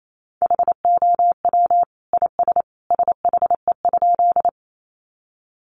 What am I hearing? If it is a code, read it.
Morse code, heard as a keyed tone, about 28 wpm, 700 Hz. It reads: HOW IS SHE?
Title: Sentences from Top 100 Words 28wpm